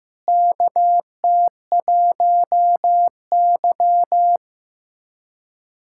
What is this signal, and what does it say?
Morse code, heard as a keyed tone, about 16 wpm, 700 Hz. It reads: KT1Y
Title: USA Callsign DE Callsign - 4 Characters - 15wpm